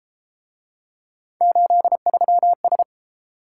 Morse code, heard as a keyed tone, about 33 wpm, 700 Hz. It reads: 83S